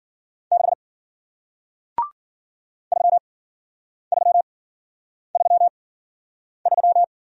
Morse code, heard as a keyed tone, about 53 wpm, spaced as for 9 wpm, 700 Hz. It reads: 6 3333